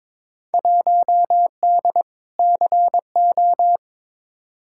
Morse code, heard as a keyed tone, about 21 wpm, 700 Hz. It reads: 1D CO
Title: Field Day Exchanges - 22wpm